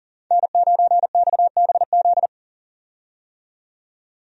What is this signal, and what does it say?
Morse code, heard as a keyed tone, about 40 wpm, 700 Hz. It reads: N9XBZ
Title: USA Call Signs - 5 Characters - 40wpm